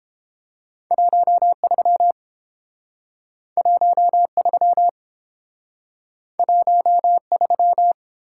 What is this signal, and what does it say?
Morse code, heard as a keyed tone, about 30 wpm, 700 Hz. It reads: 13 13 13